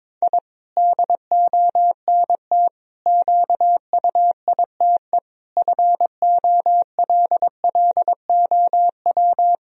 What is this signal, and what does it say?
Morse code, heard as a keyed tone, about 22 wpm, 700 Hz. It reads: I DONT QUITE FOLLOW
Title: Sentences from Top 1000 Words - Review Part 4 - 22wpm